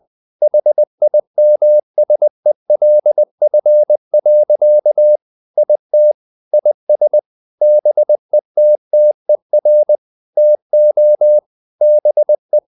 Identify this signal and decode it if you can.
Morse code, heard as a keyed tone, about 20 wpm, 600 Hz. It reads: HIMSELF. IT IS BETTER TO BE
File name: the old man and the sea part 2 20wpm